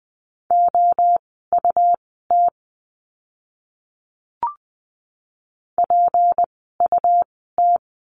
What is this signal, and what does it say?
Morse code, heard as a keyed tone, about 20 wpm, spaced as for 15 wpm, 700 Hz. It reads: OUT PUT
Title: 192 Lesson 31f - Letter v - Words 20wpm